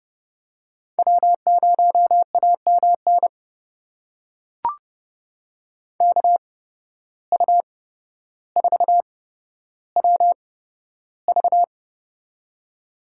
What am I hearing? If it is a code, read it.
Morse code, heard as a keyed tone, about 29 wpm, spaced as for 13 wpm, 700 Hz. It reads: W0AMN KU4WV
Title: USA Call Signs - 5 Characters - 8x Character Spacing - 30wpm